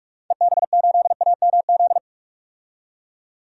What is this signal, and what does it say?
Morse code, heard as a keyed tone, about 46 wpm, 700 Hz. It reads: EB8AMZ
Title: DXCC-100 Call Signs - 6 Characters - 45wpm